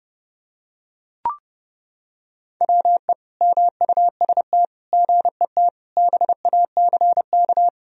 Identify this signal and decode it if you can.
Morse code, heard as a keyed tone, about 30 wpm, 700 Hz. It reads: WE MUST GET BACK
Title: Sentences from Top 1000 Words - Review Part 2 - 30wpm